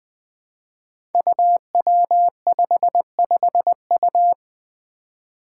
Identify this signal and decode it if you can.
Morse code, heard as a keyed tone, about 21 wpm, 700 Hz. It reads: UW55U
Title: Five Challenging Letter-Number Combinations 20wpm